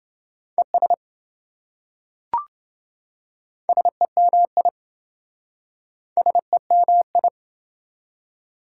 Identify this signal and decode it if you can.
Morse code, heard as a keyed tone, about 26 wpm, 700 Hz. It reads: ES SEMI SEMI